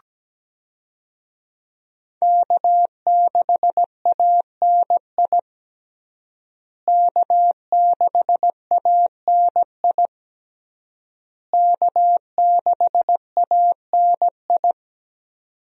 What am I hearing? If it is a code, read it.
Morse code, heard as a keyed tone, about 17 wpm, 700 Hz. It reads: K6ANI K6ANI K6ANI
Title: USA Call Signs - 5 Characters - Repeated 3x - 17wpm